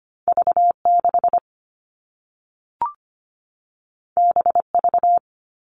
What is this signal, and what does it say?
Morse code, heard as a keyed tone, about 25 wpm, 700 Hz. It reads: V6 BV
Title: All Cut Numbers - 2-Digit 25wpm